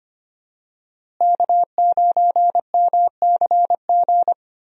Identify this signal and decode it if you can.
Morse code, heard as a keyed tone, about 25 wpm, 690 Hz. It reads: K9MCG
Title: USA Call Signs - 5 Characters - Repeated 2x - 25wpm